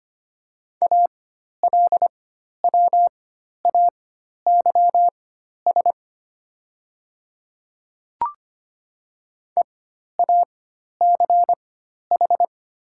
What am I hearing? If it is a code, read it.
Morse code, heard as a keyed tone, about 25 wpm, spaced as for 12 wpm, 700 Hz. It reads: ALWAYS EACH
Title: 4x Character Spacing - Top 300 Words - 25wpm